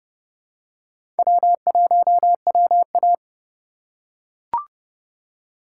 Morse code, heard as a keyed tone, about 32 wpm, 700 Hz. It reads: W1WA
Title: USA Call Signs - 4 Characters - 4x Character Spacing - 30wpm